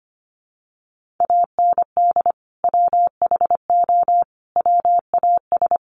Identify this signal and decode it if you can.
Morse code, heard as a keyed tone, about 25 wpm, 700 Hz. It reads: AND WHO WAS
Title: Sentences from Top 100 Words 25wpm